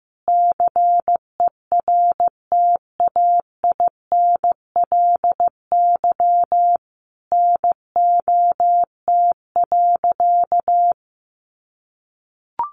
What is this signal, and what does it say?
Morse code, heard as a keyed tone, about 15 wpm, 700 Hz. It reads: CERTAINLY NOT.
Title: Sentences from Top 900 Words 15wpm